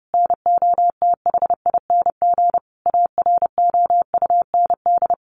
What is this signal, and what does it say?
Morse code, heard as a keyed tone, about 30 wpm, 700 Hz. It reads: NOTHING AROUND
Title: Sentences from Top 600 Words 30wpm